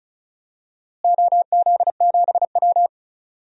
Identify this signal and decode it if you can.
Morse code, heard as a keyed tone, about 36 wpm, 700 Hz. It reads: OZ7W
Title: DXCC-100 Call Signs - 4 Characters - Repeated 2x - 35wpm